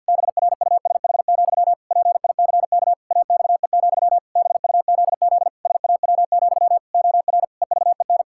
Morse code, heard as a keyed tone, about 50 wpm, 700 Hz. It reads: BRUSH, PICK AXE, BUZZ SAW, OR EVEN
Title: Pangram Challenge v2 - 50wpm